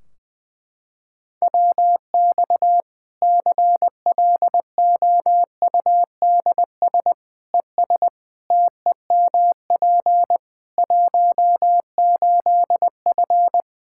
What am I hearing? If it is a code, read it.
Morse code, heard as a keyed tone, about 20 wpm, 700 Hz. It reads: WX CLOUDS ES TEMP 18F
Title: Practice QSO - Exchange 2 - Info - 20wpm